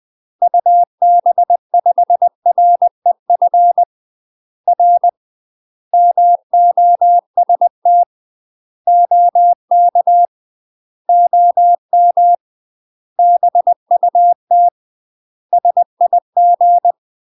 Morse code, heard as a keyed tone, about 20 wpm, 700 Hz. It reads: UB5REF R MOST OK OM BUT SIG
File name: UK MASTER QSOs TRAINING / 20WPM / UK MASTER QSOs 0040